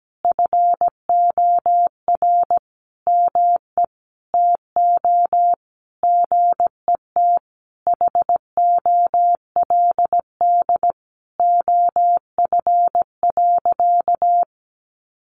Morse code, heard as a keyed tone, about 17 wpm, 700 Hz. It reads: FOR ME TO GET HOLD OF.
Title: A Sound Of Thunder 17wpm